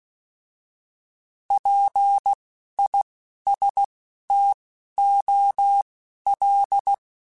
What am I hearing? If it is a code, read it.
Morse code, heard as a keyed tone, about 16 wpm, spaced as for 11 wpm, 800 Hz. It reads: PISTOL
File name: Alphabetical / Alphabetical mp3s / Ps @10